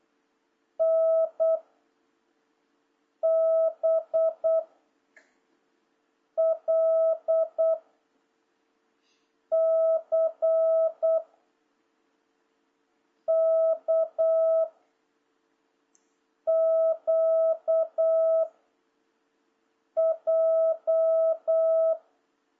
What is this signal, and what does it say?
Morse code, heard as a keyed tone, about 8 wpm, 650 Hz. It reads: N B L C K Q J
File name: jh8cbh / 2026cw